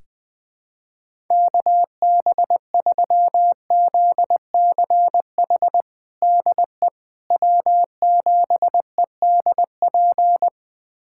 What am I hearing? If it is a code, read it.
Morse code, heard as a keyed tone, about 21 wpm, 700 Hz. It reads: KB3ZCH DE W7EDP
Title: USA Callsign DE Callsign 20wpm